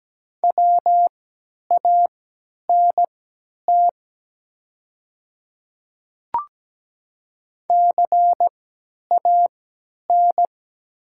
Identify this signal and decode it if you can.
Morse code, heard as a keyed tone, about 17 wpm, spaced as for 10 wpm, 700 Hz. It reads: WANT CAN
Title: 3x Character Spacing - Top 200 Words - 17wpm